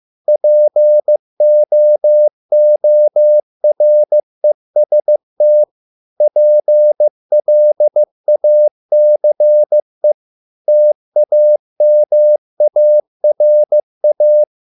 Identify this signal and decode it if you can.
Morse code, heard as a keyed tone, about 15 wpm, 600 Hz. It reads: POOREST PLACE TAMARA